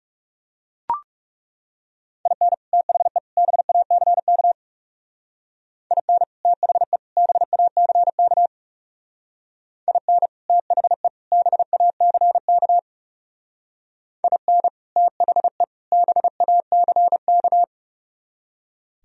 Morse code, heard as a keyed tone, about 36 wpm, 700 Hz. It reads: IN THE BACK IN THE BACK IN THE BACK IN THE BACK
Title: Speed-Racing - Sets of 3 Words 30wpm